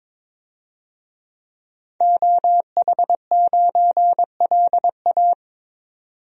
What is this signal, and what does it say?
Morse code, heard as a keyed tone, about 22 wpm, 700 Hz. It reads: OH9LA
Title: DXCC-100 Call Signs - 5 Characters - 22wpm